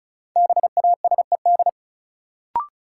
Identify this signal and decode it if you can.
Morse code, heard as a keyed tone, about 36 wpm, 700 Hz. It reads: BASED
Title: Sets of 2 Words - New Word Drill - 35wpm